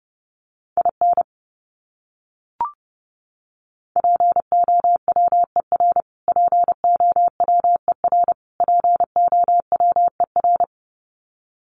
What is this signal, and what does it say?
Morse code, heard as a keyed tone, about 30 wpm, 700 Hz. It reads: IN POWER POWER POWER
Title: Top 500 Words Repeated 30wpm